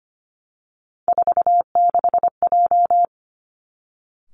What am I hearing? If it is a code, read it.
Morse code, heard as a keyed tone, about 25 wpm, 700 Hz. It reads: 46J
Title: Three Challenging Letter-Number Combinations 25wpm